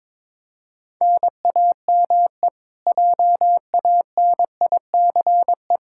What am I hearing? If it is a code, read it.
Morse code, heard as a keyed tone, about 21 wpm, 700 Hz. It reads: NAME JANICE
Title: Practice QSO - Exchange 1 - Beginning - 22wpm